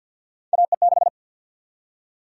Morse code, heard as a keyed tone, about 49 wpm, 700 Hz. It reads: AE6
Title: USA Call Sign Prefixes - 50wpm